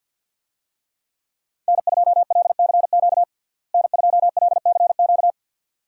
Morse code, heard as a keyed tone, about 47 wpm, 700 Hz. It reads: N2LCQ N2LCQ
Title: USA Call Signs - 5 Characters - Repeated 2x - 50wpm